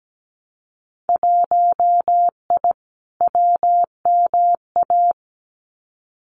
Morse code, heard as a keyed tone, about 17 wpm, 700 Hz. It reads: 1I WMA
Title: Winter Field Day Exchanges Encore 17wpm